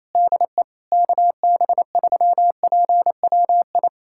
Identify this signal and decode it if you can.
Morse code, heard as a keyed tone, about 28 wpm, 700 Hz. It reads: DE KB3PWI
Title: USA Callsign DE Callsign 28wpm